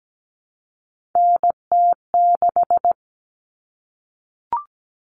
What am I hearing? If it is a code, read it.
Morse code, heard as a keyed tone, about 17 wpm, 700 Hz. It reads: NT6
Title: All Cut Numbers - 3-Digit 17wpm